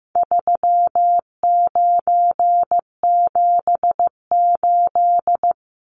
Morse code, heard as a keyed tone, about 15 wpm, 700 Hz. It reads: 3978
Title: Numbers - 4-Digit 15wpm